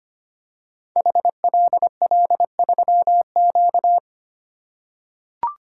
Morse code, heard as a keyed tone, about 25 wpm, 700 Hz. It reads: HLL3Q
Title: Code Groups - 5 Letter:Numbers - 25wpm